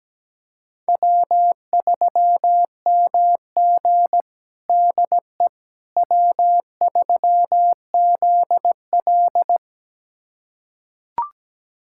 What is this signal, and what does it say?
Morse code, heard as a keyed tone, about 17 wpm, 700 Hz. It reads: W3MG DE W3ZL
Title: USA Callsign DE Callsign - 4 Characters - 17wpm